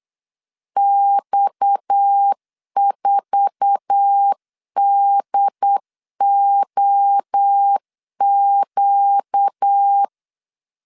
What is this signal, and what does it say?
Morse code, heard as a keyed tone, about 8 wpm, 800 Hz. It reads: X4DOQ
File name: CW / DET / 1 9 1500C S C 8wpm 5653